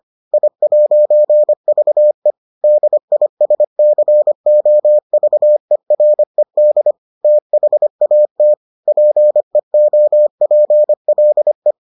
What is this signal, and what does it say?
Morse code, heard as a keyed tone, about 25 wpm, 600 Hz. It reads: I'VE DISCOVERED THAT PEOPLE